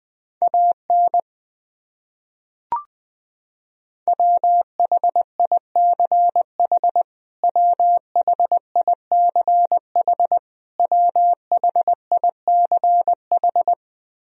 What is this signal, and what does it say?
Morse code, heard as a keyed tone, about 20 wpm, 700 Hz. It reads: AN WHICH WHICH WHICH
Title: Top 100 Words Repeated 20wpm